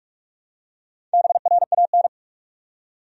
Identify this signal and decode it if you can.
Morse code, heard as a keyed tone, about 44 wpm, 700 Hz. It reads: BRAN
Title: USA Common Names - 45wpm